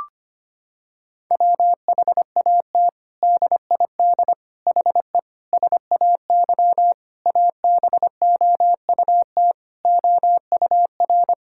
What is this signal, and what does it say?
Morse code, heard as a keyed tone, about 25 wpm, 700 Hz. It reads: WHAT DID HE SAY ABOUT OUR
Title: Sentences from Top 1000 Words - Review Part 6 - 25wpm